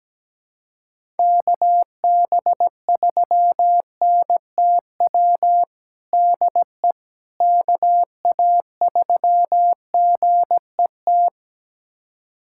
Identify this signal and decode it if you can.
Morse code, heard as a keyed tone, about 17 wpm, 700 Hz. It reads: KB3NTW DE KA3GET
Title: USA Callsign DE Callsign - 6 Characters - 17wpm